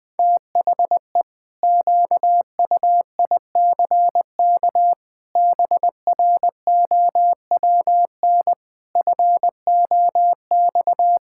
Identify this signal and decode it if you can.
Morse code, heard as a keyed tone, about 19 wpm, 700 Hz. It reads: THE QUICK BROWN FOX
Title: Pangram Challenge v2 - 20wpm